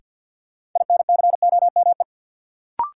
Morse code, heard as a keyed tone, about 52 wpm, 700 Hz. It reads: INCOME
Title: Sentences from Top 1000 Words - Review Part 5 - New Word Drill - 50wpm